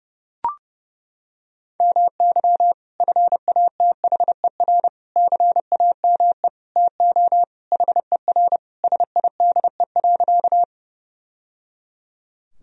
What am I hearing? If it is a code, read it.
Morse code, heard as a keyed tone, about 31 wpm, 700 Hz. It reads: MY FATHER CAME TO HER SIDE.
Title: Sentences from Top 1000 Words - Review Part 1 - 30wpm